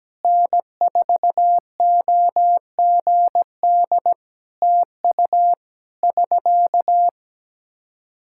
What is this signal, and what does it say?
Morse code, heard as a keyed tone, about 17 wpm, 700 Hz. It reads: N4OGD TU <SK>
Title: K1USN SST QSO Practice 17wpm